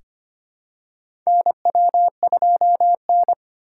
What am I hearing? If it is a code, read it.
Morse code, heard as a keyed tone, about 25 wpm, 700 Hz. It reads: NW2N
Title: USA Call Signs - 2x Character Spacing - 25wpm